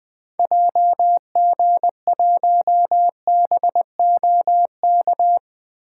Morse code, heard as a keyed tone, about 20 wpm, 700 Hz. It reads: JG1BOK
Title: 2023 SOTA Activator Call Signs - 20wpm